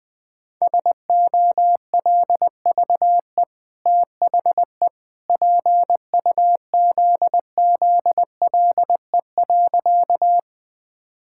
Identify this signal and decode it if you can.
Morse code, heard as a keyed tone, about 20 wpm, 700 Hz. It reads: SOLVE THE PUZZLE.
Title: Easy Sentences with Hints - 20wpm